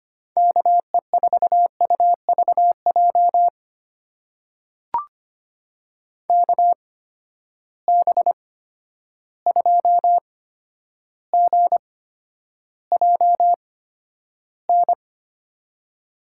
Morse code, heard as a keyed tone, about 25 wpm, spaced as for 11 wpm, 700 Hz. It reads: KE4UVJ KB2GJN
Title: USA Call Signs - 6 Characters - 8x Character Spacing - 25wpm